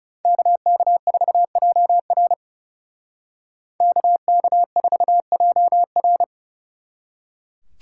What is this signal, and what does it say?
Morse code, heard as a keyed tone, about 39 wpm, 700 Hz. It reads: KK4JR KK4JR